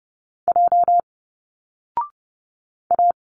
Morse code, heard as a keyed tone, about 30 wpm, 700 Hz. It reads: J A
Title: Single Letters 30wpm